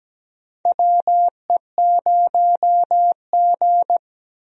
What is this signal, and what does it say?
Morse code, heard as a keyed tone, about 17 wpm, 700 Hz. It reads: WE0G
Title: USA Call Signs - 4 Characters - 17wpm